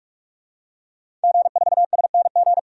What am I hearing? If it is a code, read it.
Morse code, heard as a keyed tone, about 44 wpm, 690 Hz. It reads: G4SNG
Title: DXCC-100 Call Signs - 5 Characters - 45wpm